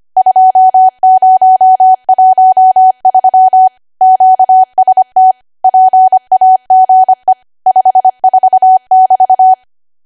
Morse code, heard as a keyed tone, about 25 wpm, 750 Hz. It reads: 2013 QST PAGE 54<BT>
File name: Archive / 25 WPM / 25WPM